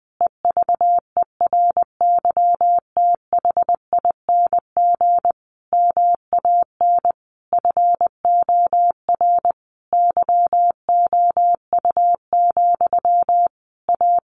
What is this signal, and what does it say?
Morse code, heard as a keyed tone, about 20 wpm, 700 Hz. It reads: EVERYTHING MAN FOR YOU, A
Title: A Sound Of Thunder 20wpm